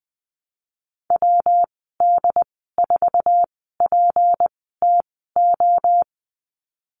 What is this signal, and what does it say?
Morse code, heard as a keyed tone, about 20 wpm, 700 Hz. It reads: W D 4 P T O
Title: 154 Lesson 25g - Letter p - Call Signs 20wpm